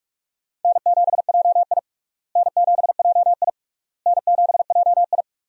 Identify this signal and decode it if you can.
Morse code, heard as a keyed tone, about 46 wpm, 700 Hz. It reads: N7JI N7JI N7JI
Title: USA Call Signs - 4 Characters - Repeated 3x - 45wpm